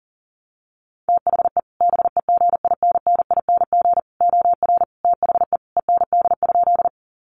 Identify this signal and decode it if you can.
Morse code, heard as a keyed tone, about 40 wpm, 700 Hz. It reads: THE BEGINNING OR THE END?
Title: Sentences from Top 800 Words 40wpm